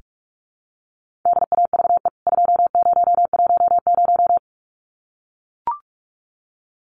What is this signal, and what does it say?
Morse code, heard as a keyed tone, about 44 wpm, 700 Hz. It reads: DAVE 2010
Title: CWT Exchange 45wpm